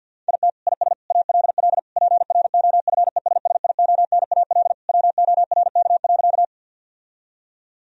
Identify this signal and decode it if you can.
Morse code, heard as a keyed tone, about 49 wpm, 700 Hz. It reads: IT IS ALL PROFESSIONAL WORK.
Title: Sentences from Top 1000 Words - Review Part 4 - 50wpm